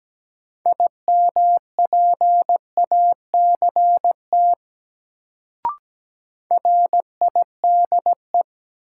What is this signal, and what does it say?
Morse code, heard as a keyed tone, about 17 wpm, 700 Hz. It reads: IMPACT RIDE
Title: Sentences from Top 1000 Words - Review Part 6 - New Word Drill - 17wpm